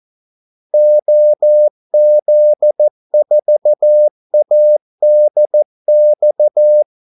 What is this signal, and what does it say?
Morse code, heard as a keyed tone, about 14 wpm, 600 Hz. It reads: OZ4ADX
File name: Callsigns-14wpm